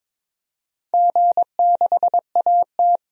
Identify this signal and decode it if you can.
Morse code, heard as a keyed tone, about 22 wpm, 700 Hz. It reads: G6AT
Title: DXCC-100 Call Signs - 4 Characters - 22wpm